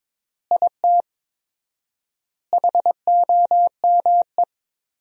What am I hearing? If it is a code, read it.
Morse code, heard as a keyed tone, about 22 wpm, 700 Hz. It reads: IT HOME
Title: 4x Word-Spacing - Sentences from Top 500 Words - 22wpm